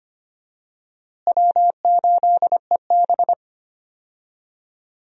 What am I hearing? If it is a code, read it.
Morse code, heard as a keyed tone, about 25 wpm, 700 Hz. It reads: W8EB
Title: USA Call Signs - 4 Characters - Repeated 3x - 25wpm